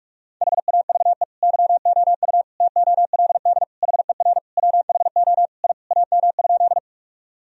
Morse code, heard as a keyed tone, about 46 wpm, 700 Hz. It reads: HAVE YOU TOLD HER WHO I AM?
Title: Sentences from Top 1000 Words - Review Part 2 - 45wpm